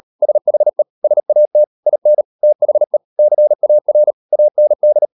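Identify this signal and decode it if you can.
Morse code, heard as a keyed tone, about 38 wpm, 600 Hz. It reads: SHE SAT IN THE CAR AND
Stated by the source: Morse code practice at 38 WPM, sending Ken Follett's novel Never in English, from CW med Gnister.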